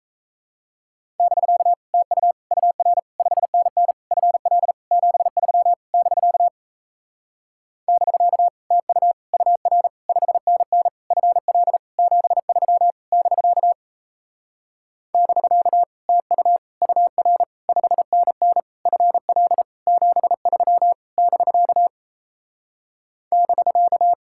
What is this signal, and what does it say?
Morse code, heard as a keyed tone, about 38 wpm, 700 Hz. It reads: <BK> TU UR 5NN FL 73 <BK> <BK> TU UR 5NN FL 73 <BK> <BK> TU UR 5NN FL 73 <BK> <BK>